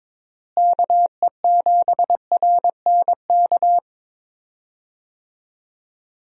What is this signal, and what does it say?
Morse code, heard as a keyed tone, about 22 wpm, 700 Hz. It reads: KE7RNK
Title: USA Call Signs - 6 Characters - Repeated 2x - 22wpm